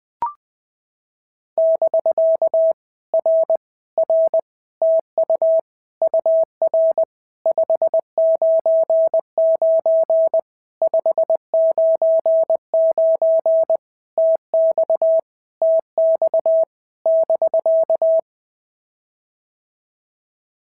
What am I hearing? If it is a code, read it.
Morse code, heard as a keyed tone, about 20 wpm, 650 Hz. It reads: <BK> R R TU UR 599 599 TX TX <BK>
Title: POTA QSO Practice - 20wpm